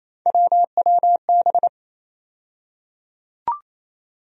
Morse code, heard as a keyed tone, about 28 wpm, 700 Hz. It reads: WWB
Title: USA Call Sign Postfixes - 3 Letters 28wpm